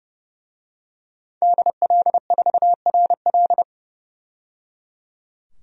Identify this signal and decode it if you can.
Morse code, heard as a keyed tone, about 32 wpm, 700 Hz. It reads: DL4RL
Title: DXCC-100 Call Signs - 5 Characters - 30wpm